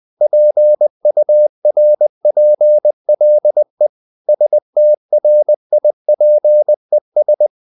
Morse code, heard as a keyed tone, about 20 wpm, 600 Hz. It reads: PURPLE STRIPES
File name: the old man and the sea part 2 20wpm